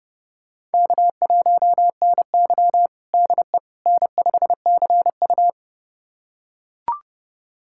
Morse code, heard as a keyed tone, about 32 wpm, 700 Hz. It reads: K1NY DE N5CU
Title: USA Callsign DE Callsign - 4 Characters - 30wpm